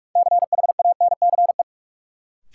Morse code, heard as a keyed tone, about 43 wpm, 700 Hz. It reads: CHANCE
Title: Sets of 3 Words - New Word Drill - 45wpm